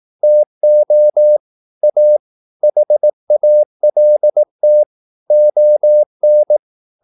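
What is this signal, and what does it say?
Morse code, heard as a keyed tone, about 18 wpm, 600 Hz. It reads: TO A HALT ON
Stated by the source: Morse practice at 18 wpm, English text, from CW med Gnister.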